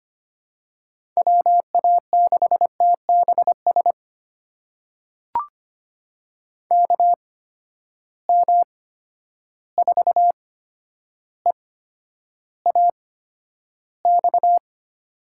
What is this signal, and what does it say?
Morse code, heard as a keyed tone, about 25 wpm, spaced as for 11 wpm, 700 Hz. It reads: WA6TBS KM4EAX
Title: USA Call Signs - 6 Characters - 8x Character Spacing - 25wpm